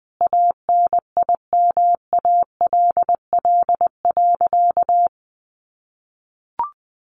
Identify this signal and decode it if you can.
Morse code, heard as a keyed tone, about 20 wpm, 700 Hz. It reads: ANIMALL.
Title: A Sound Of Thunder 20wpm